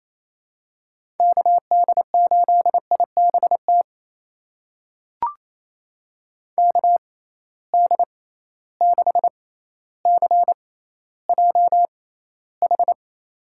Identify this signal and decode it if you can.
Morse code, heard as a keyed tone, about 28 wpm, spaced as for 14 wpm, 700 Hz. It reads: KD8IBT KD6CJH
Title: USA Call Signs - 6 Characters - 6x Character Spacing - 28wpm